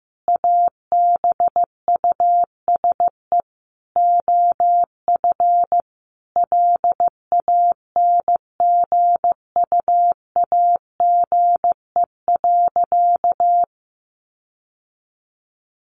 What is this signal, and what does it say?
Morse code, heard as a keyed tone, about 15 wpm, 700 Hz. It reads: ABUSE OF LANGUAGE.
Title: Sentences from Top 1300 Words 15wpm